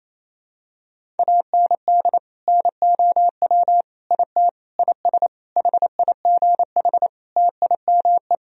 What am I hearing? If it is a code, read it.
Morse code, heard as a keyed tone, about 28 wpm, 700 Hz. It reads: AND NOW IT IS HIGH TIME
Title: Sentences from Top 1000 Words - Review Part 1 - 28wpm